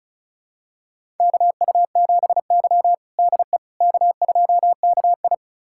Morse code, heard as a keyed tone, about 36 wpm, 700 Hz. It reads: KU7Y DE K2KI
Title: USA Callsign DE Callsign - 4 Characters - 35wpm